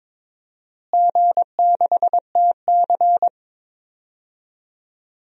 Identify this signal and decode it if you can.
Morse code, heard as a keyed tone, about 22 wpm, 700 Hz. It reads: G6TC
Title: DXCC-100 Call Signs - 4 Characters - Repeated 2x - 22wpm